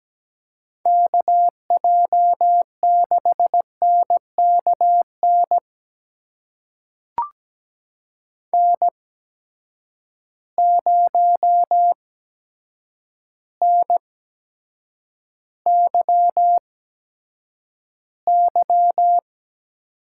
Morse code, heard as a keyed tone, about 17 wpm, spaced as for 8 wpm, 700 Hz. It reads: KJ6NKN N0NYY
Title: USA Call Signs - 8x Character Spacing - 17wpm